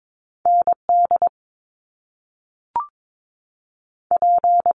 Morse code, heard as a keyed tone, about 22 wpm, 700 Hz. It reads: ND P
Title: Top Two Letter Combinations in Words 22wpm